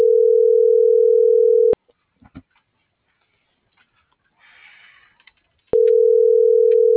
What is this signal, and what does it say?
Ringback tone (North America), 440 + 480 Hz.
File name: CENTRAL MULTIMODAL / out josimar.cruz - .76423